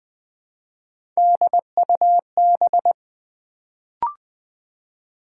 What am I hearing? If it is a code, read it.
Morse code, heard as a keyed tone, about 20 wpm, 700 Hz. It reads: DUB
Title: Build A Word - 1 Letter At A Time - 3-Letter Words - 20wpm